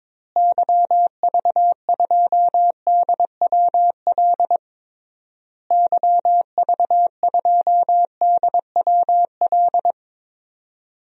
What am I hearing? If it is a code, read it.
Morse code, heard as a keyed tone, about 21 wpm, 700 Hz. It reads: YV2DWL YV2DWL
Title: DXCC-100 Call Signs - 6 Characters - Repeated 3x - 22wpm